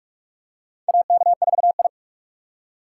Morse code, heard as a keyed tone, about 46 wpm, 700 Hz. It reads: AK4I
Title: USA Call Signs - 4 Characters - 6x Character Spacing - 45wpm